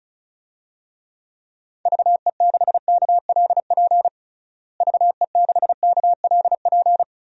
Morse code, heard as a keyed tone, about 34 wpm, 700 Hz. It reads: VE6KLP VE6KLP
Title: DXCC-100 Call Signs - 6 Characters - Repeated 2x - 35wpm